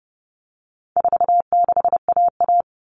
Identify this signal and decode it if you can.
Morse code, heard as a keyed tone, about 30 wpm, 700 Hz. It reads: 46AA
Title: All Cut Numbers - 4-Digit 30wpm